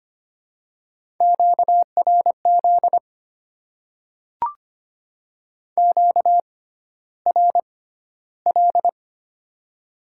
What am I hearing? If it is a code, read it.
Morse code, heard as a keyed tone, about 25 wpm, spaced as for 13 wpm, 700 Hz. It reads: QRZ QRL
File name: Top 90 QSO Elements - 6x Character Spacing - 25wpm